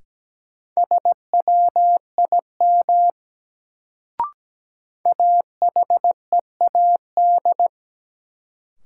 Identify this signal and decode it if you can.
Morse code, heard as a keyed tone, about 17 wpm, 700 Hz. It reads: SWIM AHEAD
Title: Sentences from Top 1000 Words - Review Part 6 - New Word Drill - 17wpm